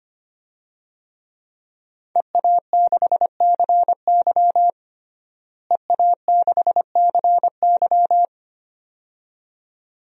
Morse code, heard as a keyed tone, about 25 wpm, 700 Hz. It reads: EA6CY EA6CY
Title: DXCC-100 Call Signs - 5 Characters - Repeated 2x - 25wpm